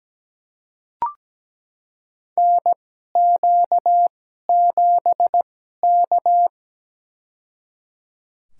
Morse code, heard as a keyed tone, about 17 wpm, spaced as for 12 wpm, 700 Hz. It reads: NQ7K
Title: USA Call Signs - 2x Character Spacing - 17wpm